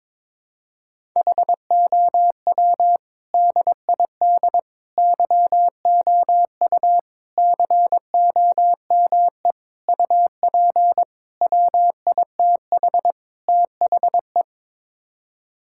Morse code, heard as a keyed tone, about 22 wpm, 700 Hz. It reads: HOW DID YOU COME UP WITH THE
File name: practice / 22wpm / Sets of 7 Words - 22wpm